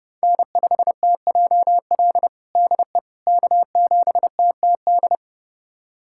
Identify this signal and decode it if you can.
Morse code, heard as a keyed tone, about 32 wpm, 700 Hz. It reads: N5TJL DE K7TTD
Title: USA Callsign DE Callsign - 5 Characters - 30wpm